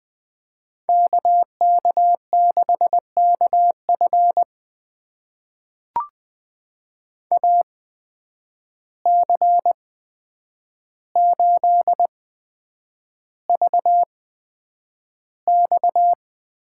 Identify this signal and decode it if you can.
Morse code, heard as a keyed tone, about 20 wpm, spaced as for 9 wpm, 700 Hz. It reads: KK6KF AC8VX
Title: USA Call Signs - 5 Characters - 8x Character Spacing - 20wpm